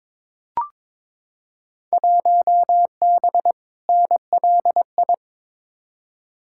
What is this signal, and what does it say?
Morse code, heard as a keyed tone, about 22 wpm, 700 Hz. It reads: 1B NLI
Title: Field Day Exchanges Encore - 22wpm